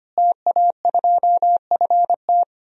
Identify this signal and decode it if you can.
Morse code, heard as a keyed tone, about 25 wpm, 700 Hz. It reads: TA2FT
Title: DXCC-100 Call Signs - 5 Characters - Repeated 2x - 25wpm